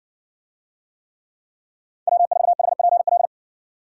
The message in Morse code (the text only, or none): W4HPL